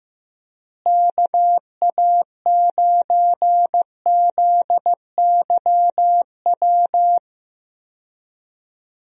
Morse code, heard as a keyed tone, about 15 wpm, 700 Hz. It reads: KA9ZYW
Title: USA Call Signs - 6 Characters - Repeated 3x - 15wpm